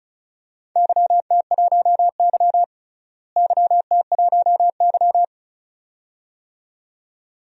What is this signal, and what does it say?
Morse code, heard as a keyed tone, about 34 wpm, 700 Hz. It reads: YT1Y YT1Y
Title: DXCC-100 Call Signs - 4 Characters - Repeated 2x - 35wpm